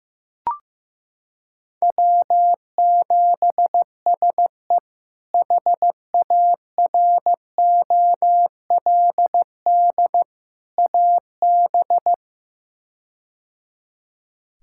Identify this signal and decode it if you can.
Morse code, heard as a keyed tone, about 16 wpm, 700 Hz. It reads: W7SE HAROLD AB
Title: K1USN SST QSO Practice - 15wpm